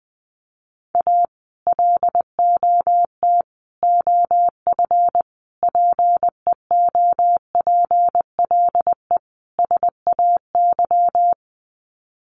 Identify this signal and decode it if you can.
Morse code, heard as a keyed tone, about 20 wpm, 700 Hz. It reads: A LOT OF PEOPLE SAY
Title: Sets of 5 Words Encore2 20wpm